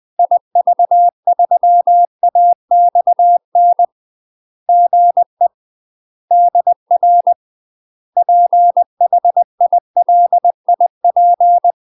Morse code, heard as a keyed tone, about 20 wpm, 700 Hz. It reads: IV3AXN GE DR PHILIP
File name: UK MASTER QSOs TRAINING / 20WPM / UK MASTER QSOs 0036